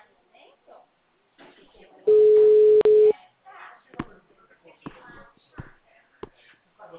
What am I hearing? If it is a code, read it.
Ringback tone (Europe, 425 Hz), with speech.
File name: out- -2004- - - .8730